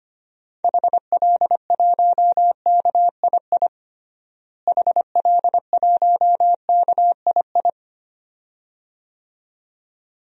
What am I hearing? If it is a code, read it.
Morse code, heard as a keyed tone, about 25 wpm, 700 Hz. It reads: HL1KII HL1KII
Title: DXCC-100 Call Signs - 6 Characters - Repeated 2x - 25wpm